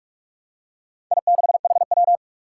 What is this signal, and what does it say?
Morse code, heard as a keyed tone, about 44 wpm, 700 Hz. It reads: I6HW
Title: Four Challenging Letter-Number Combinations 45wpm